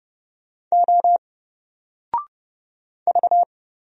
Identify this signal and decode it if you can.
Morse code, heard as a keyed tone, about 30 wpm, 700 Hz. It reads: O V
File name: Single-Letter-Number-30wpm